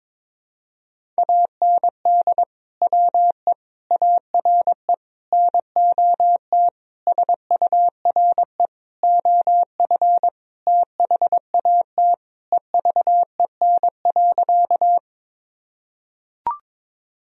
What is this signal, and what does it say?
Morse code, heard as a keyed tone, about 21 wpm, 700 Hz. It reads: AND WE ARE NOT SURE OF THAT EVEN.
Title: Sentences from Top 1000 Words - Review Part 1 - 22wpm